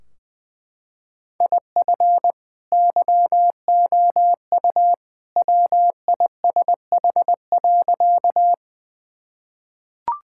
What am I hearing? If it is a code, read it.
Morse code, heard as a keyed tone, about 20 wpm, 700 Hz. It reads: IF YOU WISH.
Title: Sentences from Top 1600 Words 20wpm